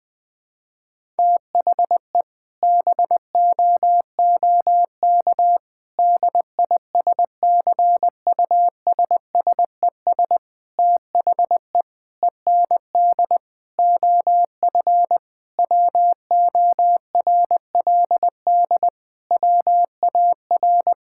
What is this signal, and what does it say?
Morse code, heard as a keyed tone, about 20 wpm, 700 Hz. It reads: THE BOOK DISCUSSES THE END OF WORLD WAR
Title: Sentences with 6-Word Phrases - 20wpm